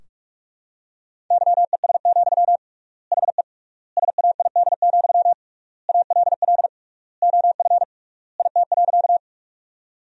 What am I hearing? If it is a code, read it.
Morse code, heard as a keyed tone, about 44 wpm, 700 Hz. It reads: YES, HE SAID, ALL OF IT.
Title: 3x Word-Spacing - Sentences from Top 500 Words - 45wpm